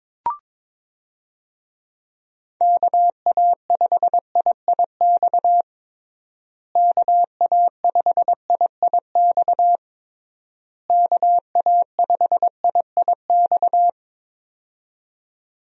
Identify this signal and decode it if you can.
Morse code, heard as a keyed tone, about 22 wpm, 700 Hz. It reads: KA5IIX KA5IIX KA5IIX
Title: USA Call Signs - 6 Characters - Repeated 3x - 22wpm